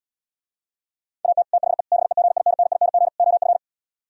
Morse code, heard as a keyed tone, about 77 wpm, 700 Hz. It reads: AT THE BEGINNING OF